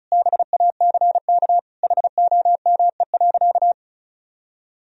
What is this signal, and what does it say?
Morse code, heard as a keyed tone, about 34 wpm, 700 Hz. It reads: BACK HOME.
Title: Sentences from Top 1000 Words - Review Part 2 - 35wpm